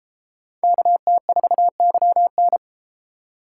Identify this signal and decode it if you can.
Morse code, heard as a keyed tone, about 31 wpm, 700 Hz. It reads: KT4YN